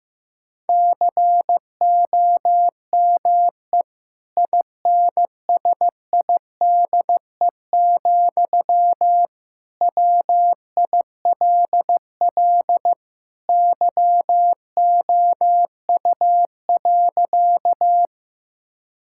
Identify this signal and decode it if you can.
Morse code, heard as a keyed tone, about 15 wpm, 700 Hz. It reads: COME INSIDE, WILL YOU.
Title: Sentences from Top 1000 Words - Review Part 6 - 15wpm